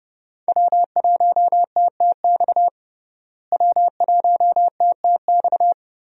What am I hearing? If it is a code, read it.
Morse code, heard as a keyed tone, about 32 wpm, 700 Hz. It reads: W1TTX W1TTX
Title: USA Call Signs - 5 Characters - Repeated 2x - 30wpm